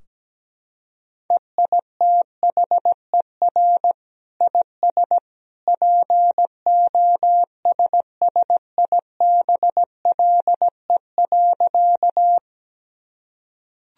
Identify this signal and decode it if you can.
Morse code, heard as a keyed tone, about 17 wpm, 700 Hz. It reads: EITHER IS POSSIBLE.
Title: Sentences from Top 1000 Words - Review Part 4 - 17wpm